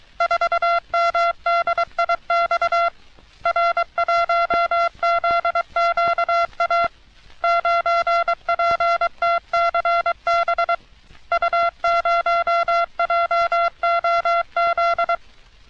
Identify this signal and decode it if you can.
Morse code, heard as a keyed tone, about 22 wpm, 680 Hz. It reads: 4MDIX R1ZQA 9PTCB U0JOZ
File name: U.S. Navy - RN-2849-124 - Sixteens Morse Code Practice